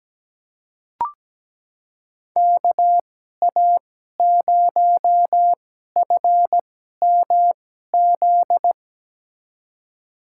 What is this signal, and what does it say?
Morse code, heard as a keyed tone, about 17 wpm, spaced as for 12 wpm, 700 Hz. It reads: KA0FMZ
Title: USA Call Signs - 2x Character Spacing - 17wpm